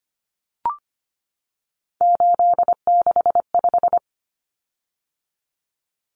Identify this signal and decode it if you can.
Morse code, heard as a keyed tone, about 25 wpm, 700 Hz. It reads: 865
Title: Common Cut Numbers - 3-Digit 25wpm